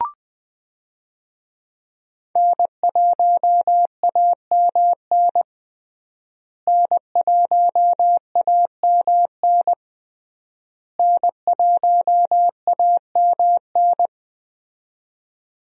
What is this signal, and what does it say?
Morse code, heard as a keyed tone, about 21 wpm, 700 Hz. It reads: N1AMN N1AMN N1AMN
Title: USA Call Signs - 5 Characters - Repeated 3x - 20wpm